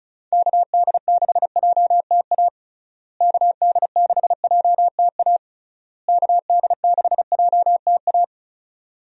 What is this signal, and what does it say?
Morse code, heard as a keyed tone, about 34 wpm, 700 Hz. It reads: KD6JTA KD6JTA KD6JTA
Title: USA Call Signs - 6 Characters - Repeated 3x - 35wpm